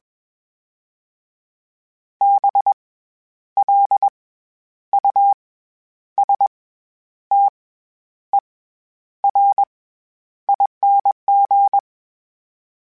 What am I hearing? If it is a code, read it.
Morse code, heard as a keyed tone, about 21 wpm, 800 Hz. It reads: B L U S T E R ING
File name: Special INGs-mp3s / Special INGs-mp3s / SlowToFaster ing @20